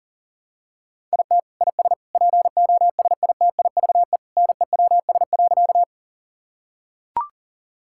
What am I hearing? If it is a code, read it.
Morse code, heard as a keyed tone, about 40 wpm, 700 Hz. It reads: IT IS POSITIVE NEWS.
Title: Sentences from Top 1000 Words - Review Part 5 - 40wpm